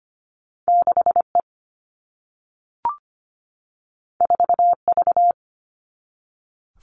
Morse code, heard as a keyed tone, about 25 wpm, 700 Hz. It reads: 6E 4V
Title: All Cut Numbers - 2-Digit 25wpm